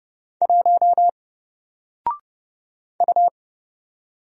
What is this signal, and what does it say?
Morse code, heard as a keyed tone, about 30 wpm, 700 Hz. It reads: 1 U
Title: Single Letter-Number 30wpm